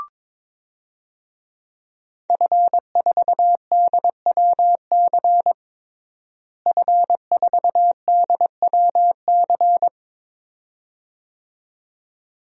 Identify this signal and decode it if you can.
Morse code, heard as a keyed tone, about 22 wpm, 700 Hz. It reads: F4DWC F4DWC
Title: DXCC-100 Call Signs - 5 Characters - Repeated 2x - 22wpm